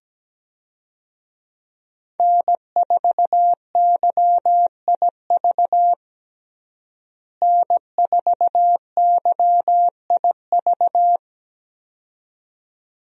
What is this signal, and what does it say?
Morse code, heard as a keyed tone, about 17 wpm, 700 Hz. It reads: N4YIV N4YIV
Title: USA Call Signs - 5 Characters - Repeated 2x - 17wpm